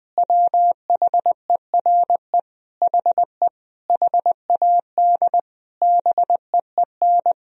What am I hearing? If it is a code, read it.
Morse code, heard as a keyed tone, about 20 wpm, 700 Hz. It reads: WHERE HE HAD BEEN
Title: Sentences from Top 1000 Words - Review Part 1 - 20wpm